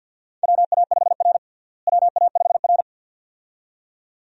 Morse code, heard as a keyed tone, about 52 wpm, 700 Hz. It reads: WA5R WA5R
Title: USA Call Signs - 4 Characters - Repeated 2x - 50wpm